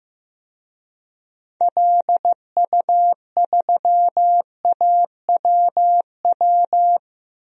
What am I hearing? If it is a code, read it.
Morse code, heard as a keyed tone, about 16 wpm, 700 Hz. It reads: LU3AWW
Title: DXCC-100 Call Signs - 6 Characters - Repeated 2x - 15wpm